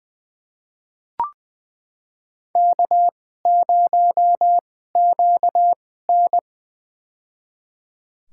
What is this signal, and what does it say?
Morse code, heard as a keyed tone, about 21 wpm, spaced as for 15 wpm, 700 Hz. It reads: K0QN
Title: USA Call Signs - 2x Character Spacing - 20wpm